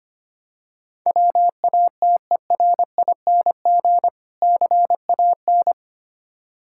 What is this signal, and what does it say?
Morse code, heard as a keyed tone, about 25 wpm, 690 Hz. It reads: WATERING CAN
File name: practice / 25wpm / Difficult Words with Hints - 25wpm